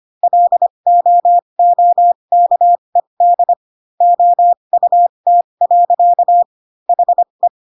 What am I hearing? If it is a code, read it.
Morse code, heard as a keyed tone, about 25 wpm, spaced as for 22 wpm, 700 Hz. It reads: LOOKED OUT. HE
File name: Pinocchio / Pinocchio ch6 18 25 ik1hge